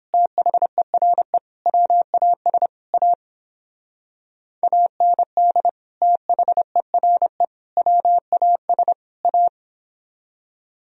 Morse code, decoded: THERE WAS A AND THERE WAS A

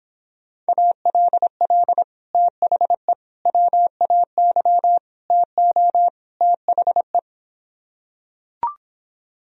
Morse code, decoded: ALL THE WAY TO THE

